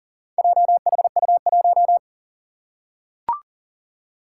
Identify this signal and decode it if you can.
Morse code, heard as a keyed tone, about 40 wpm, 700 Hz. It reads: JHU1
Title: Four Challenging Letter-Number Combinations 40wpm